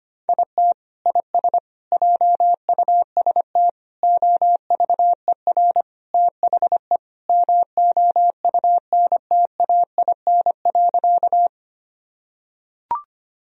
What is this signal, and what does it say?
Morse code, heard as a keyed tone, about 25 wpm, 700 Hz. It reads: IT IS JUST OVER THE MOUNTAIN.
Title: Sentences from Top 1000 Words - Review Part 1 - 25wpm